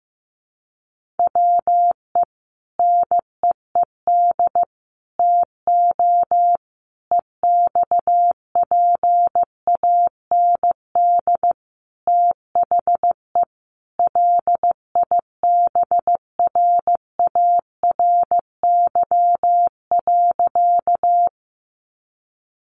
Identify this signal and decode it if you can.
Morse code, heard as a keyed tone, about 15 wpm, 700 Hz. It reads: WE NEED TO EXPAND THE LIBRARY.
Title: Sentences from Top 1400 Words 15wpm